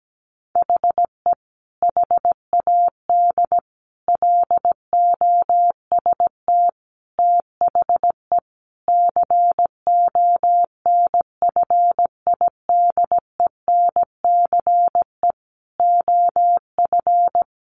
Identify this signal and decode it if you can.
Morse code, heard as a keyed tone, about 17 wpm, 700 Hz. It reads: HE HAD LOST THE CONFIDENCE OF
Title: Sentences from Top 1500 Words 17wpm